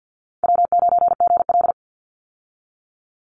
Morse code, heard as a keyed tone, about 52 wpm, 700 Hz. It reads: W9GL
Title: Call Signs Repeated 2x - 50wpm